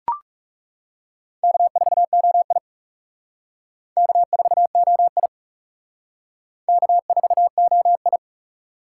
Morse code, heard as a keyed tone, about 35 wpm, 700 Hz. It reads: K4OI K4OI K4OI
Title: Speed Racing - USA 1x2 Call Signs 30wpm